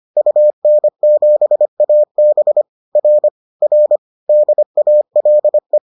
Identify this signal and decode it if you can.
Morse code, heard as a keyed tone, about 25 wpm, 600 Hz. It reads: UN7AB R R DALE